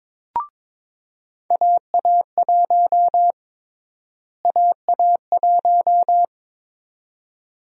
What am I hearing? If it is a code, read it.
Morse code, heard as a keyed tone, about 21 wpm, 700 Hz. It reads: AA1 AA1
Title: USA Call Sign Prefixes - Repeated 2x - 22wpm